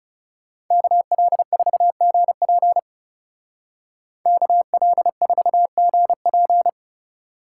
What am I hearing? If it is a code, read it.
Morse code, heard as a keyed tone, about 36 wpm, 700 Hz. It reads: KL4GP KL4GP
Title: Speed Racing - USA 2x2 Call Signs 30wpm